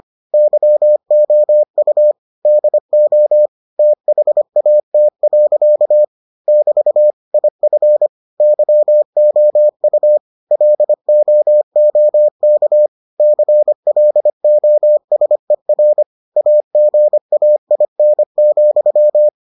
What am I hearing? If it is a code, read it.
Morse code, heard as a keyed tone, about 25 wpm, 600 Hz. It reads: YOU DO THAT. <BT> IF YOU LOOK CLOSER AGAIN,